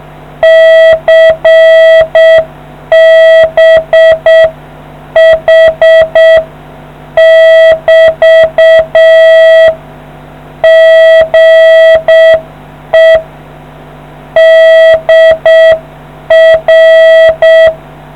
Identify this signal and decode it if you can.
Morse code, heard as a keyed tone, about 7 wpm, 650 Hz. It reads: CBH <BT> GE DR